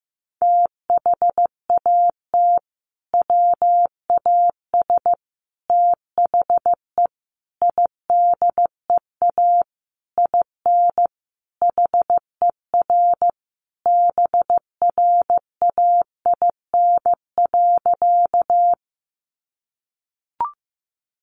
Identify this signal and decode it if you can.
Morse code, heard as a keyed tone, about 15 wpm, 700 Hz. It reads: THAT WAS THE IDEA IN HER BRAIN.
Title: Sentences from Top 1100 Words 15wpm